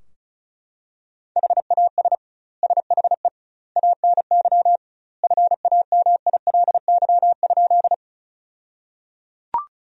Morse code, heard as a keyed tone, about 36 wpm, 700 Hz. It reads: HAS SHE ANY FAMILY?
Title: 2x Word-Spacing - Sentences from Top 400 Words - 35wpm